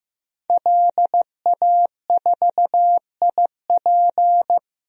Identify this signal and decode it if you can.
Morse code, heard as a keyed tone, about 15 wpm, 700 Hz. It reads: LA4IP
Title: DXCC-100 Call Signs - 5 Characters - Repeated 2x - 15wpm